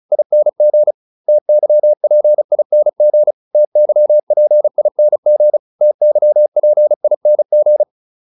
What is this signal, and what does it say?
Morse code, heard as a keyed tone, about 34 wpm, 600 Hz. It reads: ING TYPING TYPING TYPING